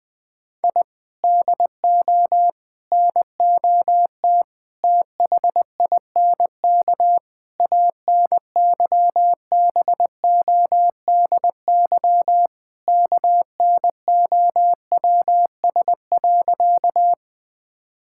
Morse code, heard as a keyed tone, about 20 wpm, 700 Hz. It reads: I DO NOT THINK ANYBODY KNOWS.
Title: Sentences with 5-Word Phrases - 20wpm